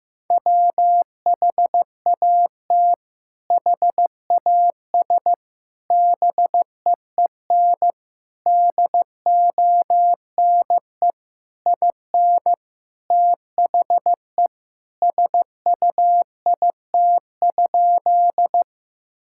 Morse code, heard as a keyed tone, about 16 wpm, 700 Hz. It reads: WHAT HAS BEEN DONE IN THE SUIT?
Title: Sentences from Top 1600 Words 15wpm